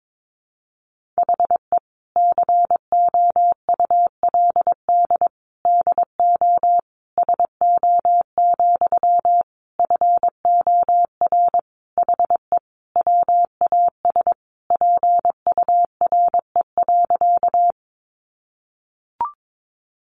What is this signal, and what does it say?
Morse code, heard as a keyed tone, about 22 wpm, 700 Hz. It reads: HE COULD DO SO, FOR HE WAS PURE.
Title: Sentences from Top 1500 Words 22wpm